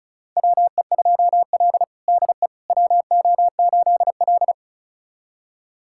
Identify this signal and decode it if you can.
Morse code, heard as a keyed tone, about 36 wpm, 700 Hz. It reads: WE2L DE WO8L
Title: USA Callsign DE Callsign - 4 Characters - 35wpm